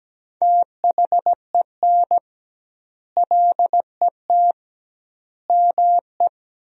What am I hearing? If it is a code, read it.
Morse code, heard as a keyed tone, about 17 wpm, 700 Hz. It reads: THEN LET ME
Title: 2x Word-Spacing - Sentences from Top 300 Words - 17wpm